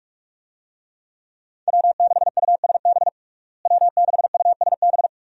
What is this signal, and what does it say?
Morse code, heard as a keyed tone, about 41 wpm, 700 Hz. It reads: W6USB W6USB
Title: USA Call Signs - 5 Characters - Repeated 2x - 45wpm